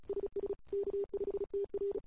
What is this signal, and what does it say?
Morse code, heard as a keyed tone, about 35 wpm, 410 Hz. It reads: SS K5TR
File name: CW / 2008-ss-cw-k5tr-op-n5rz -21:00